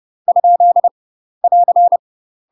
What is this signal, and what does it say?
Morse code, heard as a keyed tone, about 30 wpm, 700 Hz. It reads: ? <AR>
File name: UK MASTER QSOs TRAINING / 30WPM / UK MASTER QSOs 0030